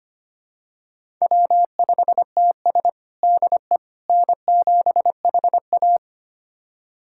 Morse code, heard as a keyed tone, about 25 wpm, 700 Hz. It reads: W5TS DE N7HA
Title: USA Callsign DE Callsign - 4 Characters - 25wpm